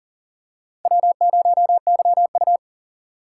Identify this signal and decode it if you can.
Morse code, heard as a keyed tone, about 38 wpm, 700 Hz. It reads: W0YU
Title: USA Call Signs - 4 Characters - 2x Character Spacing - 40wpm